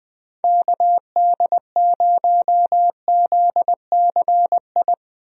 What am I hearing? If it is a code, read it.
Morse code, heard as a keyed tone, about 20 wpm, 700 Hz. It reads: KD0ZCI
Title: USA Call Signs - 6 Characters - Repeated 3x - 20wpm